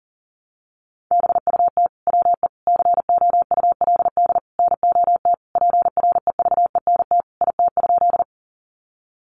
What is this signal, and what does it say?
Morse code, heard as a keyed tone, about 40 wpm, 700 Hz. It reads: BUT WE COULD NOT PREVENT IT?
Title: Sentences from Top 1100 Words 40wpm